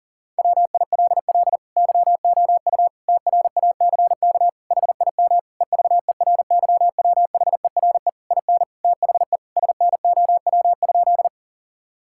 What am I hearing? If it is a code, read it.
Morse code, heard as a keyed tone, about 37 wpm, 700 Hz. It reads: WILL YOU TRACK HIM EVERYWHERE IN THE SNOW?
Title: Sentences from Top 1600 Words 40wpm